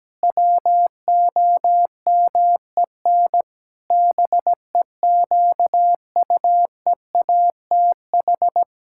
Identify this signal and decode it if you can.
Morse code, heard as a keyed tone, about 17 wpm, 700 Hz. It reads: WOMEN BEQUEATH
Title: Pangram Challenge v2 - 17wpm